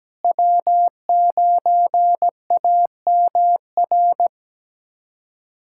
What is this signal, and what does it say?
Morse code, heard as a keyed tone, about 17 wpm, 700 Hz. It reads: W9AMR
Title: USA Callsign DE Callsign - 5 Characters - 17wpm